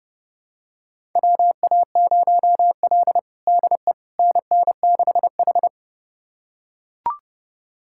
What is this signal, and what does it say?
Morse code, heard as a keyed tone, about 30 wpm, 700 Hz. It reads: WA0L DE NN6H
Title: USA Callsign DE Callsign - 4 Characters - 30wpm